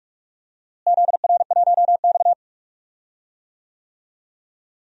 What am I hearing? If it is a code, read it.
Morse code, heard as a keyed tone, about 46 wpm, 700 Hz. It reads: ZR1X
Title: DXCC-100 Call Signs - 4 Characters - Repeated 3x - 45wpm